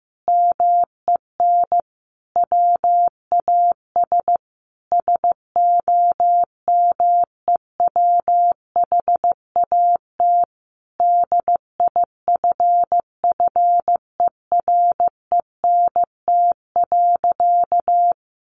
Morse code, heard as a keyed tone, about 15 wpm, 700 Hz. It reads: MEN WAS SOMEWHAT DIFFERENT.
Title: Sentences from Top 1300 Words 15wpm